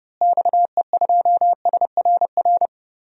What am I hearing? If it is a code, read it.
Morse code, heard as a keyed tone, about 32 wpm, 700 Hz. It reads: XE2SRR
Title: DXCC-100 Call Signs - 6 Characters - Repeated 3x - 30wpm